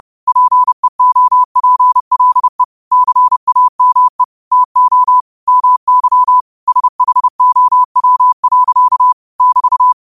Morse code, heard as a keyed tone, about 30 wpm, 1000 Hz. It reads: PEOPLE CAME TO MY SHOW. <BT>
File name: qotdfiles / 30wpm